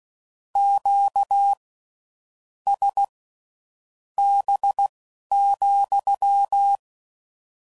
Morse code, heard as a keyed tone, about 16 wpm, spaced as for 7 wpm, 800 Hz. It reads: QSB,